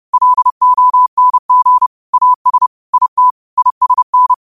Morse code, heard as a keyed tone, about 30 wpm, 1000 Hz. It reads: LONG AS IT ISN